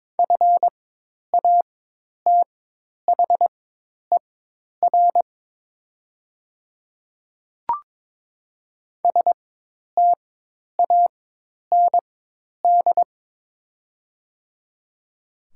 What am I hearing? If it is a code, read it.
Morse code, heard as a keyed tone, about 23 wpm, spaced as for 10 wpm, 700 Hz. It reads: FATHER STAND
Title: 4x Character Spacing - Top 300 Words - 22wpm